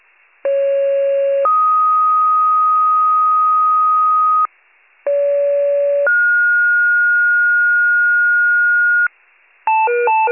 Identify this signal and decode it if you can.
Two-tone page, heard as a one-second tone, then a three-second tone, about 550 then 1250 Hz, about 570 then 1430 Hz, with noise. It is heard from 0.4 to 9.1 s.